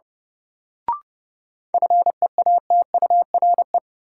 Morse code, heard as a keyed tone, about 31 wpm, 700 Hz. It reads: FEATURE
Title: Sentences from Top 800 Words - New Word Drill - 30wpm